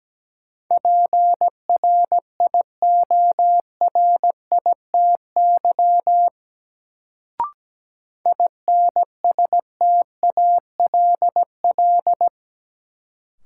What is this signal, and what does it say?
Morse code, heard as a keyed tone, about 17 wpm, 700 Hz. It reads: PRIORITY INSTALL
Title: Sentences from Top 1600 Words - New Word Drill Alt - 17wpm